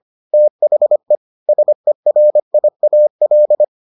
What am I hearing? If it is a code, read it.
Morse code, heard as a keyed tone, about 25 wpm, 600 Hz. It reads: THE SERIAL